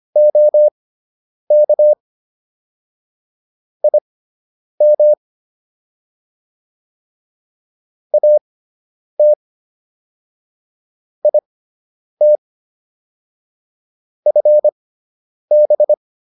Morse code, heard as a keyed tone, about 25 wpm, spaced as for 9 wpm, 600 Hz. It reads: OK IM AT IT FB